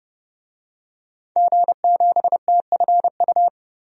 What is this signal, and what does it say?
Morse code, heard as a keyed tone, about 30 wpm, 700 Hz. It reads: G7TFU
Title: DXCC-100 Call Signs - 5 Characters - 30wpm